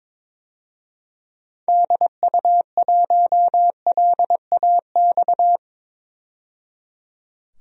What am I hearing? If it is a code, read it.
Morse code, heard as a keyed tone, about 22 wpm, 700 Hz. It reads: DU1LAX
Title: DXCC-100 Call Signs - 6 Characters - 22wpm